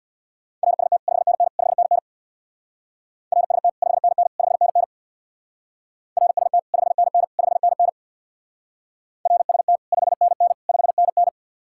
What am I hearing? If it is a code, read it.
Morse code, heard as a keyed tone, about 59 wpm, 700 Hz. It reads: RST 5NN 5NN RST 5NN 5NN RST 5NN 5NN RST 5NN 5NN
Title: Speed-Racing Common QSO Phrases Encore - 50wpm